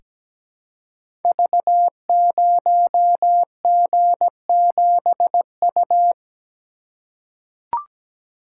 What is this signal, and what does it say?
Morse code, heard as a keyed tone, about 17 wpm, 700 Hz. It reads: V0G7U
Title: Code Groups - 5 Letter:Numbers - 17wpm